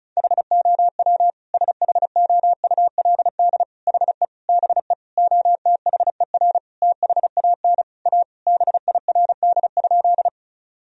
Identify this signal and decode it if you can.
Morse code, heard as a keyed tone, about 34 wpm, 700 Hz. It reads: HOW SHOULD HE BE OTHER THAN A BIRD?
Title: Sentences from Top 1000 Words - Review Part 2 - 35wpm